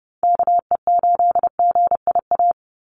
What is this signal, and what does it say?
Morse code, heard as a keyed tone, about 30 wpm, 700 Hz. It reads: KE8GIA
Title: USA Call Signs 30wpm